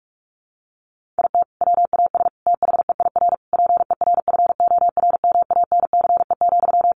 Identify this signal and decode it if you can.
Morse code, heard as a keyed tone, about 44 wpm, 700 Hz. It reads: IT WAS THEIR PERFORMANCE,
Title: Sentences from Top 1500 Words 45wpm